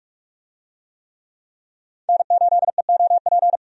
Morse code, heard as a keyed tone, about 42 wpm, 700 Hz. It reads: N8EOP
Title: USA Call Signs - 5 Characters - 45wpm